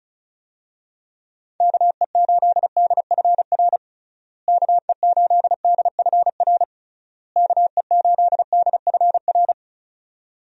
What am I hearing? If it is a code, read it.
Morse code, heard as a keyed tone, about 34 wpm, 700 Hz. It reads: KE8DFR KE8DFR KE8DFR
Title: USA Call Signs - 6 Characters - Repeated 3x - 35wpm